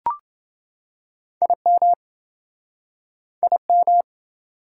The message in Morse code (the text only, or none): IM IM